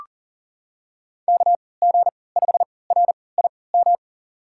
Morse code, heard as a keyed tone, about 40 wpm, spaced as for 23 wpm, 700 Hz. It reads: KG5RIM
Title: USA Call Signs - 6 Characters - 3x Character Spacing - 40wpm